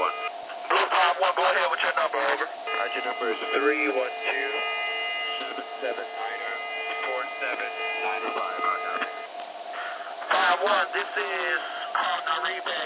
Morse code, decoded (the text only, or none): none